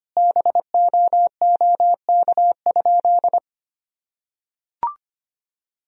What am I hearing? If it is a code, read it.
Morse code, heard as a keyed tone, about 25 wpm, 700 Hz. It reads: BOOK?
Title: Sentences from Top 1000 Words - Review Part 1 - 25wpm